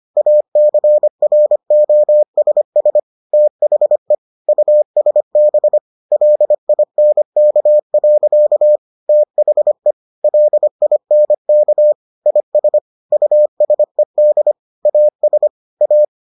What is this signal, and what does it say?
Morse code, heard as a keyed tone, about 25 wpm, 600 Hz. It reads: ACROSS THE USB LINK. THE LINK IS USED AS A